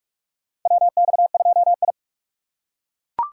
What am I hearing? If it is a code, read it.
Morse code, heard as a keyed tone, about 42 wpm, 700 Hz. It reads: WX2I
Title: USA Call Signs - 4 Characters - 4x Character Spacing - 45wpm